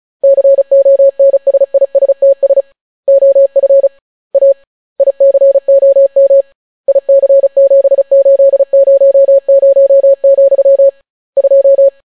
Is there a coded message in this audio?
Yes: Morse code from 0.2 to 11.9 s, heard as a keyed tone, about 36 wpm, 550 Hz. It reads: CONSISTS OF A ICOM IC7800, 2